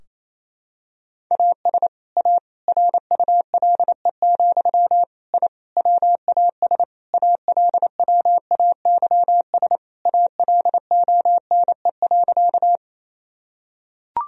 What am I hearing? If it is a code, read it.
Morse code, heard as a keyed tone, about 28 wpm, 700 Hz. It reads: AS A RULE, I WAS ALWAYS ALONE.
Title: Sentences from Top 1000 Words - Review Part 2 - 28wpm